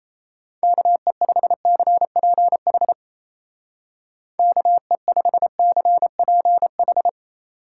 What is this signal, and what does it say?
Morse code, heard as a keyed tone, about 35 wpm, 700 Hz. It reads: KE5CPH KE5CPH
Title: Speed Racing - USA 2x3 Call Signs 28wpm